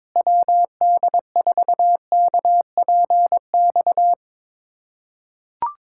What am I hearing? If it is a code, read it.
Morse code, heard as a keyed tone, about 22 wpm, 700 Hz. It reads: WD4KPX
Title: USA Callsign DE Callsign 22wpm